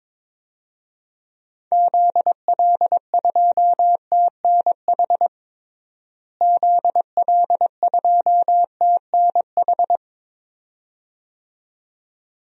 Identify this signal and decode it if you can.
Morse code, heard as a keyed tone, about 22 wpm, 700 Hz. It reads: ZL2TNH ZL2TNH
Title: DXCC-100 Call Signs - 6 Characters - Repeated 2x - 22wpm